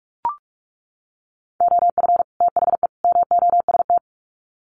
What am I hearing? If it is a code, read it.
Morse code, heard as a keyed tone, about 44 wpm, 700 Hz. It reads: OF THE MOST
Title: Sets of 3 Words 45wpm